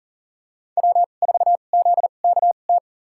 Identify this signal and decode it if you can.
Morse code, heard as a keyed tone, about 40 wpm, spaced as for 29 wpm, 700 Hz. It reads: W4ZKT
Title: USA Call Signs - 5 Characters - 2x Character Spacing - 40wpm